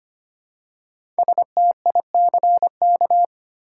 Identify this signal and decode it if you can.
Morse code, heard as a keyed tone, about 25 wpm, 700 Hz. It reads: STICK
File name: 5 Letter Words - First 3 Letters Given - 25wpm